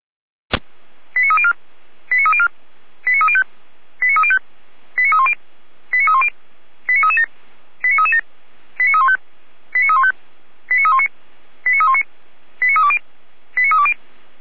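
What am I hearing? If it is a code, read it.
Five-tone selective call (ZVEI), heard as quick runs of five tones, tones of about 70 ms. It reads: 89304 89304 89305 89305 89310 89310 89337 89337 89315 89315 89319 89319 89320 89320
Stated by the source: ZVEI five-tone call 89294 received on SDR.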